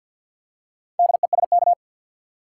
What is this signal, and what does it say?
Morse code, heard as a keyed tone, about 48 wpm, 700 Hz. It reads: DESK
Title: Four Letter Words - 50wpm